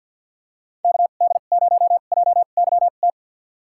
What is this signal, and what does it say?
Morse code, heard as a keyed tone, about 47 wpm, spaced as for 35 wpm, 690 Hz. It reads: KD0JYT
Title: USA Call Signs - 2x Character Spacing - 50wpm